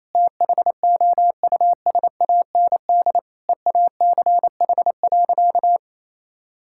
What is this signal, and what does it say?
Morse code, heard as a keyed tone, about 28 wpm, 700 Hz. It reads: THOUSAND EACH.
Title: Sentences from Top 1000 Words - Review Part 5 - 28wpm